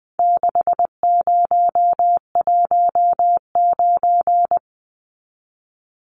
Morse code, heard as a keyed tone, about 20 wpm, 700 Hz. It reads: 6019
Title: Numbers - 4-Digit 20wpm